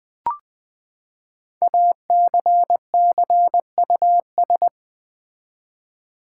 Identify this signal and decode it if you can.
Morse code, heard as a keyed tone, about 20 wpm, 700 Hz. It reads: ACCUS
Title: Build A Word - 5 Letters At A Time - 10-Letter Words - 20wpm